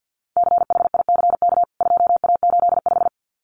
Morse code, heard as a keyed tone, about 50 wpm, 700 Hz. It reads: CHICK 2A85
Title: CWT Exchange 50wpm